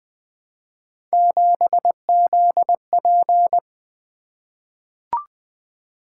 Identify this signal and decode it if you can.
Morse code, heard as a keyed tone, about 20 wpm, 700 Hz. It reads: 7ZP
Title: Three Letter-Number Combinations - 20wpm